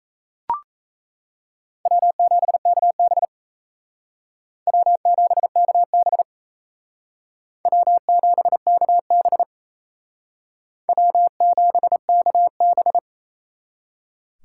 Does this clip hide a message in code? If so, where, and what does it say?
Yes: Morse code from 1.9 to 13.0 s, heard as a keyed tone, about 37 wpm, 700 Hz. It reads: W7KB W7KB W7KB W7KB